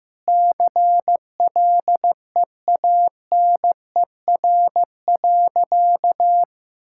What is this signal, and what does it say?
Morse code, heard as a keyed tone, about 16 wpm, 700 Hz. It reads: CLEANER.
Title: Pangram Challenge v2 - 15wpm